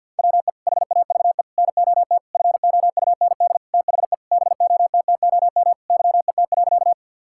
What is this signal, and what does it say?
Morse code, heard as a keyed tone, about 50 wpm, 700 Hz. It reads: WE HAVE NOT FOUND THE BOTTOM YET.
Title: Sentences from Top 1000 Words - Review Part 6 - 50wpm